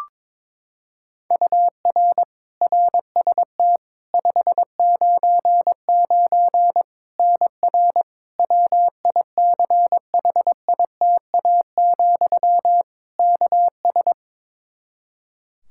Morse code, heard as a keyed tone, about 22 wpm, 700 Hz. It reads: UR RST 599 NR WICHITA, KS
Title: K1USN SST QSO Practice 22wpm